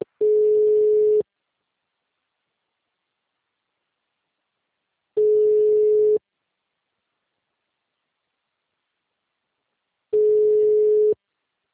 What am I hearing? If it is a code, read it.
Ringback tone (Europe, 425 Hz).
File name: out- -2015- - - .12